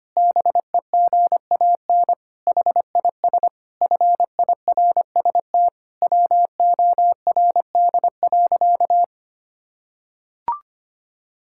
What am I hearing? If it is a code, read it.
Morse code, heard as a keyed tone, about 25 wpm, 700 Hz. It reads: BEGAN HIS FIRST WORD.
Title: Pangram Challenge v2 - 25wpm